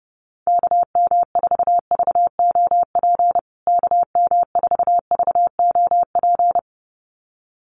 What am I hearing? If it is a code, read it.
Morse code, heard as a keyed tone, about 30 wpm, 700 Hz. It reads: KM4VOP KM4VOP
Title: USA Call Signs Repeated 2x 30wpm